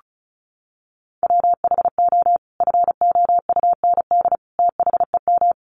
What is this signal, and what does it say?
Morse code, heard as a keyed tone, about 34 wpm, 700 Hz. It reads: WHO FOUND THEM
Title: Sentences from Top 300 Words 35wpm